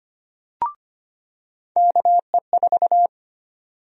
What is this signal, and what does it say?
Morse code, heard as a keyed tone, about 25 wpm, 700 Hz. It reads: KE4
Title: USA Call Sign Prefixes - 25wpm